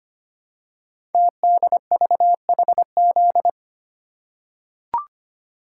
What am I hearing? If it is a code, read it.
Morse code, heard as a keyed tone, about 25 wpm, 700 Hz. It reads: TDVHZ
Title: Code Groups - 5 Letters - 25wpm